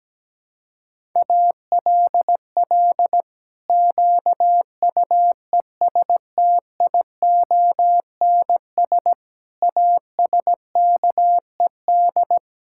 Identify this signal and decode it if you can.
Morse code, heard as a keyed tone, about 17 wpm, 700 Hz. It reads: ALL QUESTIONS ASKED
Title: Pangram Challenge v2 - 17wpm